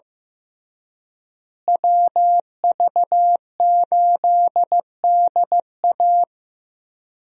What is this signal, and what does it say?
Morse code, heard as a keyed tone, about 15 wpm, 700 Hz. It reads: WV8DA
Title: USA Call Signs - 5 Characters - Repeated 3x - 15wpm